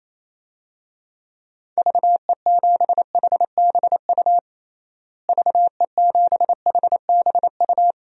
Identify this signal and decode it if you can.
Morse code, heard as a keyed tone, about 28 wpm, 700 Hz. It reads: VE7HBU VE7HBU
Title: DXCC-100 Call Signs - 6 Characters - Repeated 2x - 28wpm